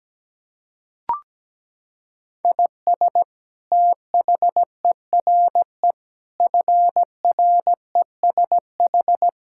Morse code, heard as a keyed tone, about 17 wpm, 700 Hz. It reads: IS THERE FRESH
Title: Sentences from Top 1600 Words 17wpm